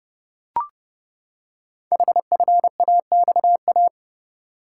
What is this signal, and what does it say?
Morse code, heard as a keyed tone, about 30 wpm, 690 Hz. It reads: HFAXA
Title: Code Groups - 5 Letters - 30wpm